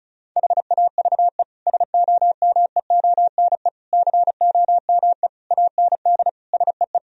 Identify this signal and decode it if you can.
Morse code, heard as a keyed tone, about 34 wpm, 700 Hz. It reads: HAVE SOMEONE COME AND SEE
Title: Sentences from Top 1000 Words - Review Part 4 - 35wpm